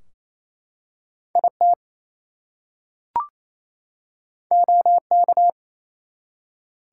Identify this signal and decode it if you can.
Morse code, heard as a keyed tone, about 27 wpm, 700 Hz. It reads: IT OK
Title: Two Letter Words - 28wpm